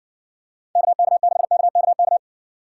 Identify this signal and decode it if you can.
Morse code, heard as a keyed tone, about 60 wpm, 700 Hz. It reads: KK6KKK